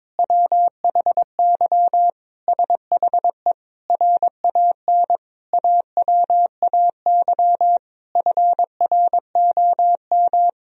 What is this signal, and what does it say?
Morse code, heard as a keyed tone, about 22 wpm, 700 Hz. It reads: WHY SHE RAN AWAY FROM
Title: Sentences from Top 1000 Words - Review Part 3 - 22wpm